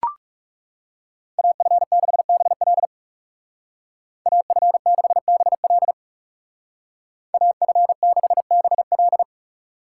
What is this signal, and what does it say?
Morse code, heard as a keyed tone, about 36 wpm, 700 Hz. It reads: AF6BL AF6BL AF6BL
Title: Speed Racing - USA 2x2 Call Signs 30wpm